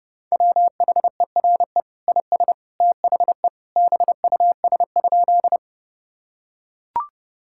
Morse code, heard as a keyed tone, about 32 wpm, 700 Hz. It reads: WHERE IS THE BUS?
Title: Sentences from Top 1000 Words - Review Part 6 - 30wpm